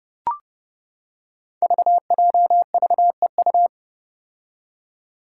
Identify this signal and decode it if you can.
Morse code, heard as a keyed tone, about 30 wpm, 700 Hz. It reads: VJVEU
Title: Five Challenging Letter-Number Combinations 30wpm